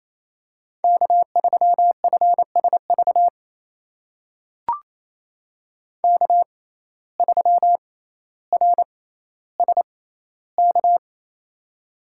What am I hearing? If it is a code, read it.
Morse code, heard as a keyed tone, about 28 wpm, spaced as for 14 wpm, 700 Hz. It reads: K3FSV K3RSK
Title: USA Call Signs - 5 Characters - 6x Character Spacing - 28wpm